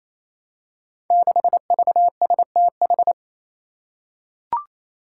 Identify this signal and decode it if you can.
Morse code, heard as a keyed tone, about 27 wpm, 700 Hz. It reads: 6VSTH
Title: Five Challenging Letter-Number Combinations 28wpm